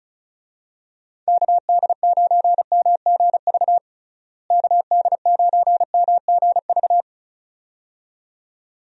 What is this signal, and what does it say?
Morse code, heard as a keyed tone, about 36 wpm, 700 Hz. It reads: KD9MGV KD9MGV
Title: USA Call Signs - 6 Characters - Repeated 2x - 35wpm